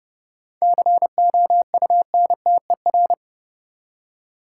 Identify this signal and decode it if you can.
Morse code, heard as a keyed tone, about 30 wpm, 700 Hz. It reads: COUNTER
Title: Seven Letter Words - 30wpm